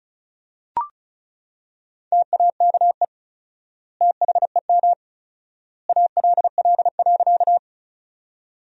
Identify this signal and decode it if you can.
Morse code, heard as a keyed tone, about 36 wpm, 700 Hz. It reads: TAKE THEM ALL.
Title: 4x Word-Spacing - Sentences from Top 200 Words - 35wpm